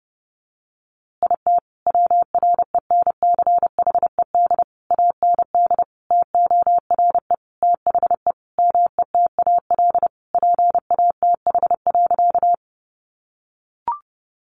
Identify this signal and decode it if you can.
Morse code, heard as a keyed tone, about 30 wpm, 700 Hz. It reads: IT WRENCHED AND TORE THE METAL PATH.
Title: A Sound Of Thunder 30wpm